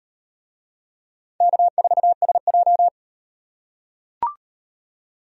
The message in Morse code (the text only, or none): K4SJ